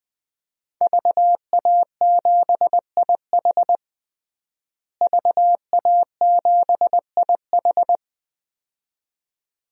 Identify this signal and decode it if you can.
Morse code, heard as a keyed tone, about 20 wpm, 700 Hz. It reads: VA7IH VA7IH
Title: DXCC-100 Call Signs - 5 Characters - Repeated 2x - 20wpm